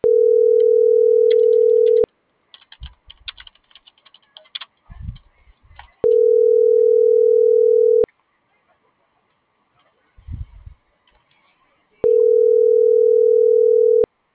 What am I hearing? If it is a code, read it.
Ringback tone (North America), 440 + 480 Hz.